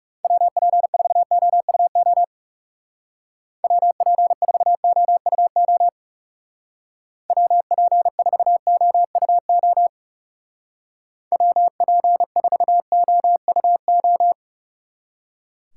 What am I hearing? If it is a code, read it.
Morse code, heard as a keyed tone, about 38 wpm, 700 Hz. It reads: WP4OUO WP4OUO WP4OUO WP4OUO